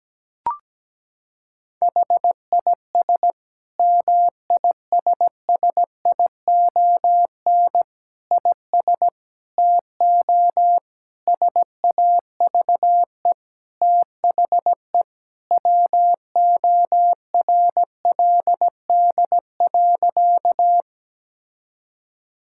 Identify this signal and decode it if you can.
Morse code, heard as a keyed tone, about 17 wpm, 700 Hz. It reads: HIS MISSION IS TO SAVE THE WORLD.
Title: Sentences from Top 1600 Words 17wpm